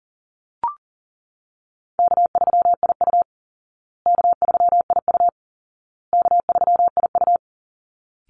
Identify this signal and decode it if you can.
Morse code, heard as a keyed tone, about 40 wpm, 700 Hz. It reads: K3IU K3IU K3IU
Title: Call Signs Repeated 2x - 40wpm